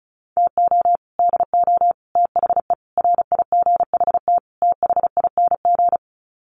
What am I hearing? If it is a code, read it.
Morse code, heard as a keyed tone, about 34 wpm, 700 Hz. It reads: TO DO THE RIGHT THING
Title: Sets of 5 Words 35wpm